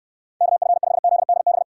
Morse code, heard as a keyed tone, about 69 wpm, 700 Hz. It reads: KF5ZDB